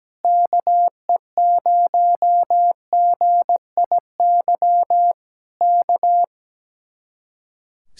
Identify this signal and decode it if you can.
Morse code, heard as a keyed tone, about 17 wpm, 700 Hz. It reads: KE0GIY K
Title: Speed-Racing Common QSO Phrases - 17wpm